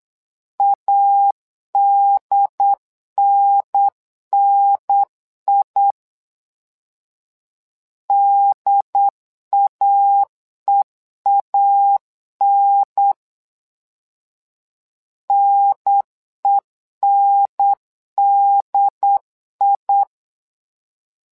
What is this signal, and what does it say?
Morse code, heard as a keyed tone, about 8 wpm, 800 Hz. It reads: ADNNI DAEAN NENDI
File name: LEZIONI / 2 UHDN / 2 UHDN 8wpm2232